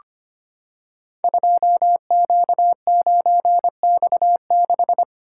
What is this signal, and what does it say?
Morse code, heard as a keyed tone, about 25 wpm, 700 Hz. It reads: 2Q9X6
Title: Code Groups - 5 Letter:Numbers - 25wpm